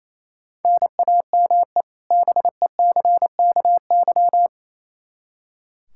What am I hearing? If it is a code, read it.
Morse code, heard as a keyed tone, about 28 wpm, 700 Hz. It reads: NAME BECKY
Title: Practice QSO - Exchange 1 - Beginning - 28wpm